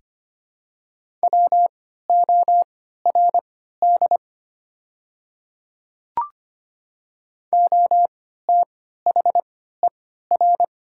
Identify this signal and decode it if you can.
Morse code, heard as a keyed tone, about 25 wpm, spaced as for 14 wpm, 700 Hz. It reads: WORD OTHER
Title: 3x Character Spacing - Top 100 Words - 25wpm